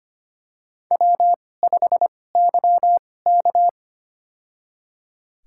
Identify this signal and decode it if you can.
Morse code, heard as a keyed tone, about 25 wpm, spaced as for 18 wpm, 700 Hz. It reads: W5YK
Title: USA Call Signs - 4 Characters - 2x Character Spacing - 25wpm